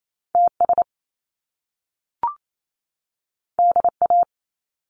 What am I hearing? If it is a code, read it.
Morse code, heard as a keyed tone, about 28 wpm, 700 Hz. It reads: TS DA
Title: Top Two Letter Combinations in Words 28wpm